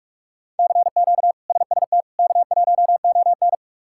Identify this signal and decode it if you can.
Morse code, heard as a keyed tone, about 46 wpm, 700 Hz. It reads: CQ SST K1ON
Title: K1USN SST QSO Practice - 45wpm